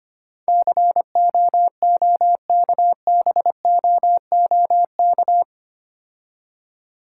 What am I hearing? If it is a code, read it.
Morse code, heard as a keyed tone, about 25 wpm, 700 Hz. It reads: COOKBOOK
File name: practice / 25wpm / Difficult Words with Hints - 25wpm